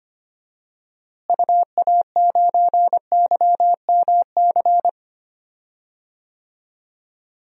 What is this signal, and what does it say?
Morse code, heard as a keyed tone, about 25 wpm, 700 Hz. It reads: UA9YMC
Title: DXCC-100 Call Signs - 6 Characters - Repeated 2x - 25wpm